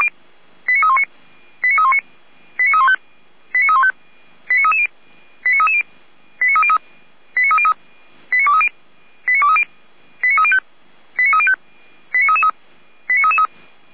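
Five-tone selective call (ZVEI), heard as quick runs of five tones, tones of about 70 ms. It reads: 89319 89319 89315 89315 89330 89330 89303 89303 89230 89230 89305 89305 89333 89333
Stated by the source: ZVEI five-tone call received by SDR.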